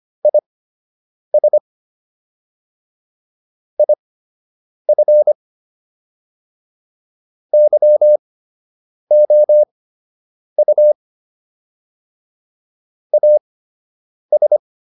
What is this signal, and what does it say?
Morse code, heard as a keyed tone, about 25 wpm, spaced as for 8 wpm, 600 Hz. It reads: IS IF YOU AS